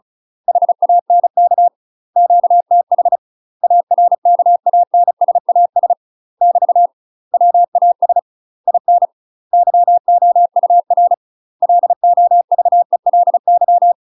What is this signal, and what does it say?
Morse code, heard as a keyed tone, about 34 wpm, 700 Hz. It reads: HANK QTH ARKANSAS <BT> WAS IN YOUR LOVELY
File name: UK MASTER QSOs TRAINING / 35WPM / UK MASTER QSOs 0034